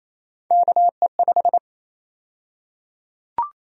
Morse code, heard as a keyed tone, about 27 wpm, 700 Hz. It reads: KE5
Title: USA Call Sign Prefixes - Repeated 2x - 28wpm